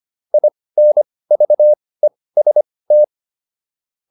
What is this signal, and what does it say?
Morse code, heard as a keyed tone, about 25 wpm, spaced as for 18 wpm, 600 Hz. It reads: INVEST